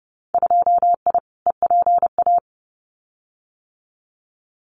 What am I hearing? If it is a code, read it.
Morse code, heard as a keyed tone, about 30 wpm, 700 Hz. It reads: 2I EPA
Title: Winter Field Day Exchanges Encore 30wpm